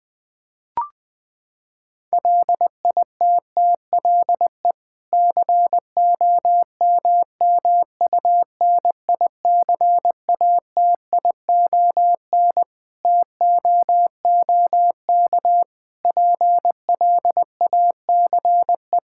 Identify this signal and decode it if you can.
Morse code, heard as a keyed tone, about 20 wpm, 700 Hz. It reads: LITTLE COMMUNICATION TOOK PLACE
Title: Sentences from Top 1000 Words - Review Part 6 - 20wpm